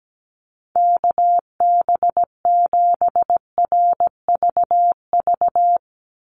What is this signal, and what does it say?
Morse code, heard as a keyed tone, about 17 wpm, 700 Hz. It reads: KB7RVV
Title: USA Call Signs 17wpm